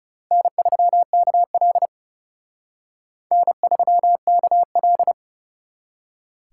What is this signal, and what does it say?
Morse code, heard as a keyed tone, about 35 wpm, 700 Hz. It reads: N3KL N3KL
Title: Speed Racing - USA 1x2 Call Signs 30wpm